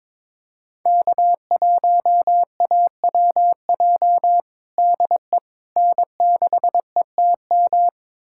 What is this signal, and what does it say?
Morse code, heard as a keyed tone, about 22 wpm, 700 Hz. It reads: K1AWJ DE N6ETM
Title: USA Callsign DE Callsign 22wpm